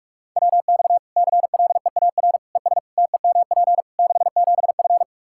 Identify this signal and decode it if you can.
Morse code, heard as a keyed tone, about 46 wpm, 700 Hz. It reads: WX CLEAR ES TEMP 67F
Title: K1USN SST QSO Practice 45wpm